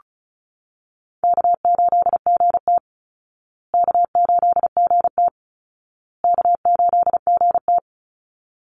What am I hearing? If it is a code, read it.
Morse code, heard as a keyed tone, about 34 wpm, 700 Hz. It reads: K8GT K8GT K8GT
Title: Easy Call Signs Repeated 3x - 35wpm